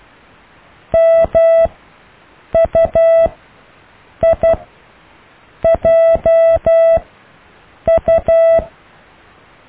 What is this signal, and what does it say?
Morse code, heard as a keyed tone, about 12 wpm, spaced as for 6 wpm, 660 Hz. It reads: MUIJU
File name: cw / IJLMU